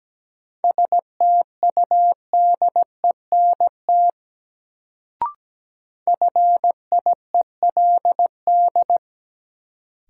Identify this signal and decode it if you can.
Morse code, heard as a keyed tone, about 17 wpm, 700 Hz. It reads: STUDENT FIELD
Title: Sentences from Top 1000 Words - New Word Drill - 17wpm